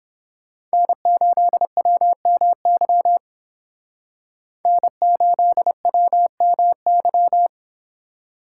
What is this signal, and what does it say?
Morse code, heard as a keyed tone, about 30 wpm, 700 Hz. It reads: N8WMY N8WMY